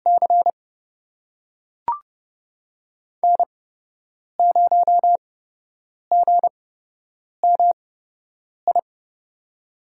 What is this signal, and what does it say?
Morse code, heard as a keyed tone, about 31 wpm, spaced as for 8 wpm, 700 Hz. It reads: C N0GMI
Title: USA Call Signs - 5 Characters - 8x Character Spacing - 30wpm